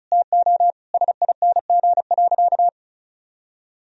Morse code, heard as a keyed tone, about 34 wpm, 700 Hz. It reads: TO SING.
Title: Sentences from Top 1000 Words - Review Part 2 - 35wpm